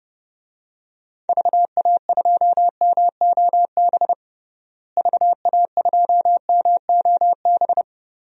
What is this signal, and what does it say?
Morse code, heard as a keyed tone, about 32 wpm, 700 Hz. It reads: VA2MOB VA2MOB
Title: DXCC-100 Call Signs - 6 Characters - Repeated 3x - 30wpm